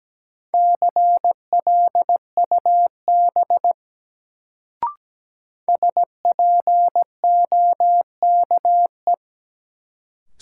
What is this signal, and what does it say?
Morse code, heard as a keyed tone, about 17 wpm, 700 Hz. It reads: CLUB SPOKE
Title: Sentences from Top 1200 Words - New Word Drill - 17wpm